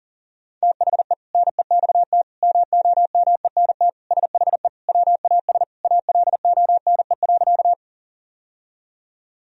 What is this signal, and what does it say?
Morse code, heard as a keyed tone, about 40 wpm, 700 Hz. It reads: THE NEXT MOMENT SHE WAS ALONE.
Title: Sentences from Top 1000 Words - Review Part 1 - 40wpm